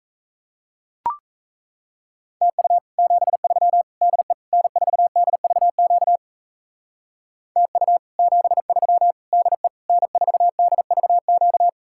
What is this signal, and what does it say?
Morse code, heard as a keyed tone, about 34 wpm, 700 Hz. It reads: TU 73 DE N4DVQ TU 73 DE N4DVQ
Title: Speed-Racing Common QSO Phrases - 28wpm